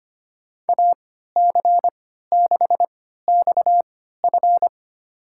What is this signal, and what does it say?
Morse code, heard as a keyed tone, about 25 wpm, spaced as for 14 wpm, 700 Hz. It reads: AC6XF
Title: USA Call Signs - 5 Characters - 3x Character Spacing - 25wpm